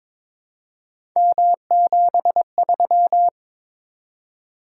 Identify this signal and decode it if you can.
Morse code, heard as a keyed tone, about 22 wpm, 700 Hz. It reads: M73
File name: practice / 22wpm / Three Letter-Number Combinations - 22wpm